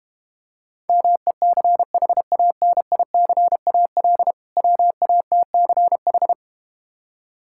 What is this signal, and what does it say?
Morse code, heard as a keyed tone, about 32 wpm, 700 Hz. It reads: MECHANICAL WATCH